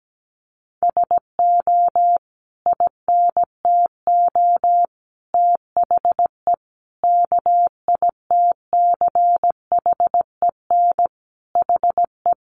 Morse code, heard as a keyed tone, about 17 wpm, 700 Hz. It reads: SO INTO THE KITCHEN HE
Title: Sentences from Top 1400 Words 17wpm